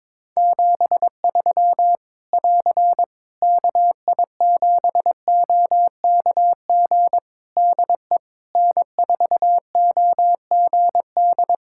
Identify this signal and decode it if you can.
Morse code, heard as a keyed tone, about 22 wpm, 700 Hz. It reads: 73 <AR> KI7OKG DE N4OGD
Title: K1USN SST QSO Practice 22wpm